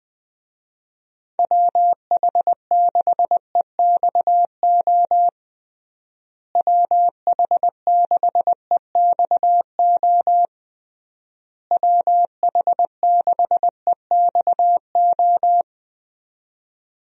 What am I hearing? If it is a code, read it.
Morse code, heard as a keyed tone, about 20 wpm, 700 Hz. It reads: WH6EXO WH6EXO WH6EXO
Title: USA Call Signs - 6 Characters - Repeated 3x - 20wpm